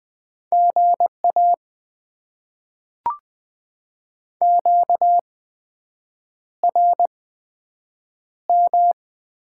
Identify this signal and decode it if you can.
Morse code, heard as a keyed tone, about 20 wpm, spaced as for 7 wpm, 700 Hz. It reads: GA QRM
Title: Top 90 QSO Elements - 8x Character Spacing - 20wpm